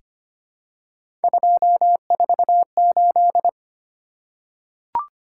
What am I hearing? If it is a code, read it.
Morse code, heard as a keyed tone, about 25 wpm, 700 Hz. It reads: 248
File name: Three Letter-Number Combinations - 25wpm